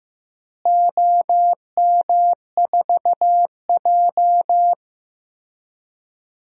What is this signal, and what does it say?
Morse code, heard as a keyed tone, about 15 wpm, 700 Hz. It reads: OM4J
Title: DXCC-100 Call Signs - 4 Characters - Repeated 3x - 15wpm